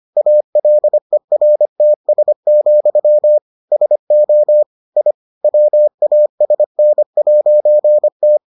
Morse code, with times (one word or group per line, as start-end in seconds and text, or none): 0.2-3.4 ALERTS,
3.7-4.6 SO
5.0-5.1 I
5.4-8.4 WASN'T